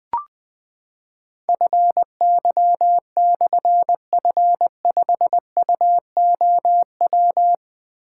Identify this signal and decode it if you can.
Morse code, heard as a keyed tone, about 20 wpm, 700 Hz. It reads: FY/F5UOW
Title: Difficult Call Signs 20wpm